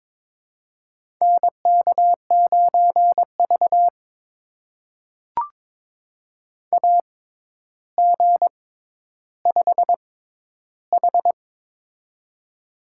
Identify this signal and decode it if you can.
Morse code, heard as a keyed tone, about 22 wpm, spaced as for 11 wpm, 700 Hz. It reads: NK9V AG5H
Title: USA Call Signs - 4 Characters - 6x Character Spacing - 22wpm